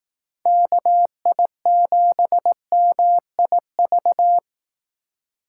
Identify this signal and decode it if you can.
Morse code, heard as a keyed tone, about 18 wpm, 700 Hz. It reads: KI7MIV
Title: Speed Racing - USA 2x3 Call Signs 15wpm